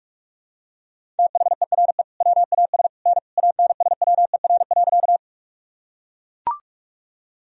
Morse code, heard as a keyed tone, about 41 wpm, 700 Hz. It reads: THERE WAS N ANSWER.
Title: Sentences from Top 300 Words and Abbreviations 45wpm